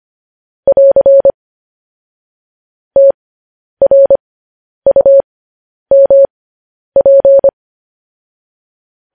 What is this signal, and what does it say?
Morse code, heard as a keyed tone, about 25 wpm, spaced as for 10 wpm, 550 Hz. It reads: <AR> TRUMP
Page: news-headlines-in-morse-code-at-10-wpm